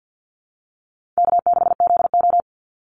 Morse code, heard as a keyed tone, about 48 wpm, 700 Hz. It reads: K6ZO
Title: Easy Call Signs Repeated 3x - 50wpm